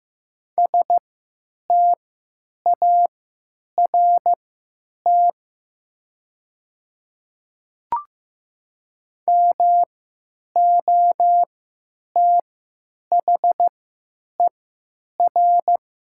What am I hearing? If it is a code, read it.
Morse code, heard as a keyed tone, about 16 wpm, spaced as for 9 wpm, 700 Hz. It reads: START MOTHER
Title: 3x Character Spacing - Top 300 Words - 15wpm